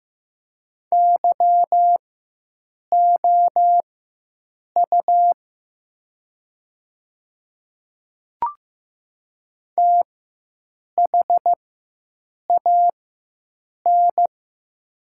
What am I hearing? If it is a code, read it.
Morse code, heard as a keyed tone, about 15 wpm, spaced as for 7 wpm, 700 Hz. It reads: YOU THAN
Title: 4x Character Spacing - Top 100 Words - 15wpm